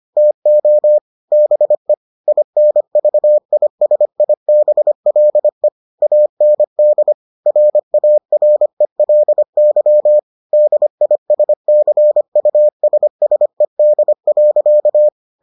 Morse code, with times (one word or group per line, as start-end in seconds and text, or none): 0.2-1.0 TO
1.3-1.9 BE
2.3-5.7 INVISIBLE
6.0-7.1 AND
7.5-10.2 RARELY
10.5-15.1 DISCUSSED.